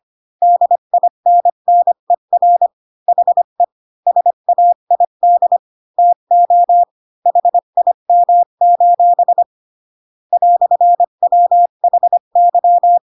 Morse code, heard as a keyed tone, about 25 wpm, 700 Hz. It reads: DINNER HE SAID TO HIM: "WHY
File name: Pinocchio / Pinocchio ch8 20 25 ik1hge